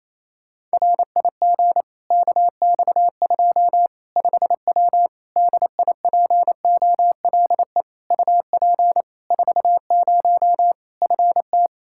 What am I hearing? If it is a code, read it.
Morse code, heard as a keyed tone, about 28 wpm, 700 Hz. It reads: RIG KX2 5W DIPOLE UP 40 FT
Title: Practice QSO - Exchange 2 - Info - 28wpm